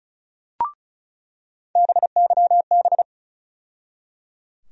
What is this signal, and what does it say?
Morse code, heard as a keyed tone, about 34 wpm, 700 Hz. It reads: BYB
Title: USA Call Sign Postfixes - 3 Letters 35wpm